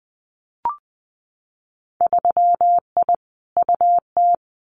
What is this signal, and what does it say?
Morse code, heard as a keyed tone, about 19 wpm, 700 Hz. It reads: 3I UT
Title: Winter Field Day Exchanges Encore 20wpm